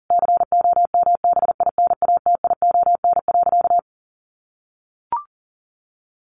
Morse code, heard as a keyed tone, about 37 wpm, 700 Hz. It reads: COMBINATION.
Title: Sentences from Top 1300 Words 40wpm